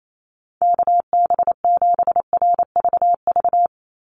Morse code, heard as a keyed tone, about 28 wpm, 700 Hz. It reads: KB7RVV
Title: USA Call Signs 28wpm